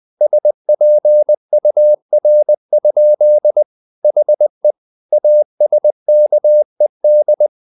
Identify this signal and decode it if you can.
Morse code, heard as a keyed tone, about 20 wpm, 600 Hz. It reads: SPUR? HE ASKED
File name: the old man and the sea / the old man and the sea part 1 20wpm